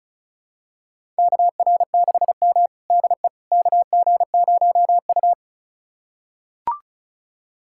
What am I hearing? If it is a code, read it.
Morse code, heard as a keyed tone, about 36 wpm, 700 Hz. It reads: KR6M DE KG0U
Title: USA Callsign DE Callsign - 4 Characters - 35wpm